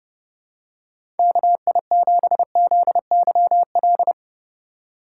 Morse code, heard as a keyed tone, about 30 wpm, 700 Hz. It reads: KI7ZYL
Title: USA Call Signs - 6 Characters - 2x Character Spacing - 30wpm